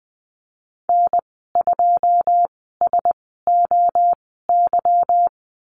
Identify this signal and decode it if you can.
Morse code, heard as a keyed tone, about 20 wpm, spaced as for 14 wpm, 700 Hz. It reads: N2SOY
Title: 167 Lesson 27g - Number 7 - Call Signs 20wpm